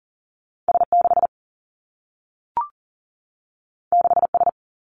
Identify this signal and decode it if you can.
Morse code, heard as a keyed tone, about 40 wpm, 700 Hz. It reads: S 6 6 S
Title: Two Challenging Letter-Number Combinations 40wpm